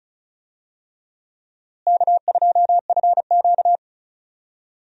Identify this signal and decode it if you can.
Morse code, heard as a keyed tone, about 34 wpm, 700 Hz. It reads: K2FQ
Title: USA Call Signs - 4 Characters - Repeated 3x - 35wpm